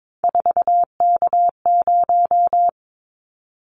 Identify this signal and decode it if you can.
Morse code, heard as a keyed tone, about 22 wpm, 700 Hz. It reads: 4K0
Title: Callsign Prefixes for Top 100 DXCC - 22wpm